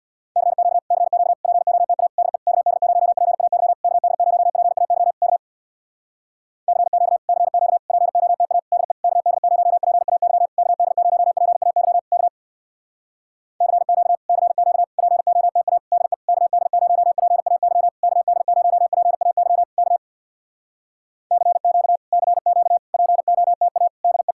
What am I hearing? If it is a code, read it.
Morse code, heard as a keyed tone, about 65 wpm, 700 Hz. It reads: CQ CQ POTA DE KD0PAY KD0PAY K CQ CQ POTA DE KD0PAY KD0PAY K CQ CQ POTA DE KD0PAY KD0PAY K CQ CQ POTA DE